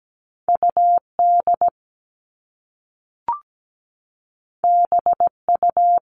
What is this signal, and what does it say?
Morse code, heard as a keyed tone, about 17 wpm, 700 Hz. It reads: UD BU
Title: All Cut Numbers - 2-Digit 17wpm